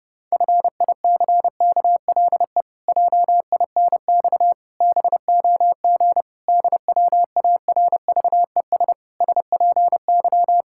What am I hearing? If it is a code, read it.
Morse code, heard as a keyed tone, about 30 wpm, 700 Hz. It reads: FICKLE JINX BOG DWARVES SPY
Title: Pangram Challenge v2 - 30wpm